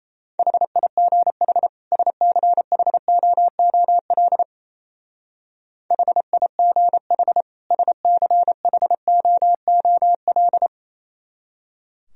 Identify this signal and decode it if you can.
Morse code, heard as a keyed tone, about 32 wpm, 690 Hz. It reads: HIGH SCHOOL HIGH SCHOOL
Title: Speed-Racing - Sets of 2 Words 28wpm